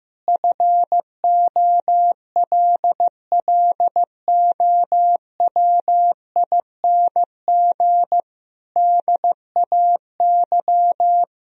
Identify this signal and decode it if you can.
Morse code, heard as a keyed tone, about 15 wpm, 700 Hz. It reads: FOLLOWING DAY
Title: Sentences from Top 1000 Words - Review Part 4 - 15wpm